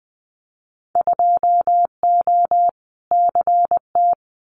Morse code, heard as a keyed tone, about 20 wpm, 700 Hz. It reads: 2O CT
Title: Winter Field Day Exchanges 20wpm